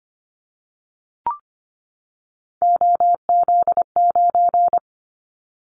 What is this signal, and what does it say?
Morse code, heard as a keyed tone, about 25 wpm, 700 Hz. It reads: OZ9
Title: Callsign Prefixes for Top 100 DXCC - 25wpm